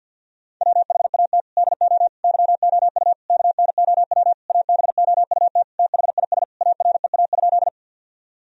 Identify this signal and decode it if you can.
Morse code, heard as a keyed tone, about 47 wpm, 700 Hz. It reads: WHAT DO YOU KNOW ABOUT THIS AREA?
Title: Sentences from Top 1000 Words - Review Part 2 - 50wpm